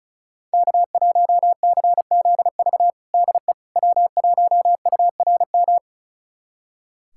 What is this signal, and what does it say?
Morse code, heard as a keyed tone, about 36 wpm, 700 Hz. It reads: K1CZV DE W1URM
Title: USA Callsign DE Callsign - 5 Characters - 35wpm